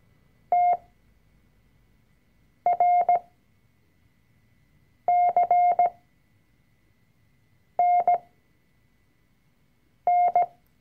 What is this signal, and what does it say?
Morse code, heard as a keyed tone, about 17 wpm, spaced as for 4 wpm, 710 Hz. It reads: TRCNN
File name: CW-Online-Kurs / Anfänger / Lektion 004